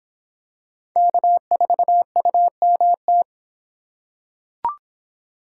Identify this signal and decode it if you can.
Morse code, heard as a keyed tone, about 24 wpm, 700 Hz. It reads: K4UMT